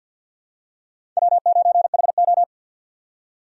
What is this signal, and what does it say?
Morse code, heard as a keyed tone, about 52 wpm, 700 Hz. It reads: W9HO
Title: USA Call Signs - 4x Character Spacing - 50wpm